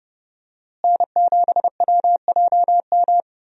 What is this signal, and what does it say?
Morse code, heard as a keyed tone, about 31 wpm, 700 Hz. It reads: N7WJM
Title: USA Call Signs - 5 Characters - Repeated 2x - 30wpm